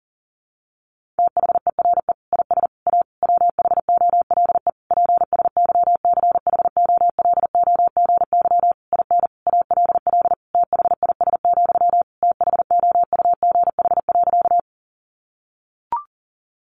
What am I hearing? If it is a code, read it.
Morse code, heard as a keyed tone, about 40 wpm, 700 Hz. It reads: THERE IS A WHOLE PSYCHOLOGY IN ALL THIS, THOUGH.
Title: Sentences from Top 1400 Words 40wpm